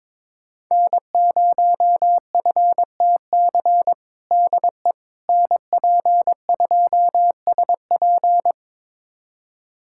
Morse code, heard as a keyed tone, about 21 wpm, 700 Hz. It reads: N0FTC DE NP2SP
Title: USA Callsign DE Callsign - 5 Characters - 22wpm